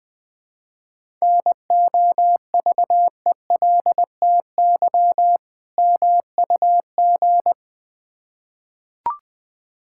Morse code, heard as a keyed tone, about 21 wpm, 700 Hz. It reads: NOVELTY MUG
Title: Difficult Words with Hints - 20wpm